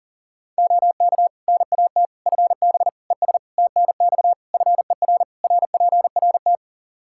Morse code, heard as a keyed tone, about 40 wpm, 700 Hz. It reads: OK NAT FB ES TNX FER RPRT
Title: Practice QSO - Exchange 2 - Info - 40wpm